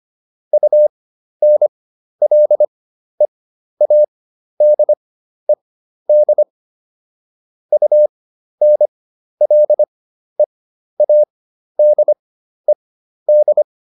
Morse code, heard as a keyed tone, about 25 wpm, spaced as for 12 wpm, 600 Hz. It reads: UNLEADED UNLEADED